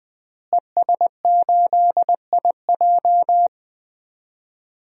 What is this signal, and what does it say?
Morse code, heard as a keyed tone, about 20 wpm, 700 Hz. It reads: ES8IJ
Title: DXCC-100 Call Signs - 5 Characters - Repeated 2x - 20wpm